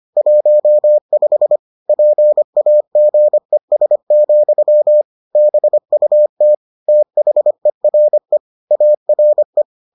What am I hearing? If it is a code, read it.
Morse code, heard as a keyed tone, about 25 wpm, 600 Hz. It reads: 15 PAGES, BUT THERE ARE